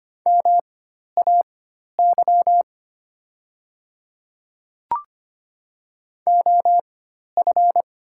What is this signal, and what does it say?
Morse code, heard as a keyed tone, about 25 wpm, spaced as for 12 wpm, 690 Hz. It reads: MAY OF
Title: 4x Character Spacing - Top 100 Words - 25wpm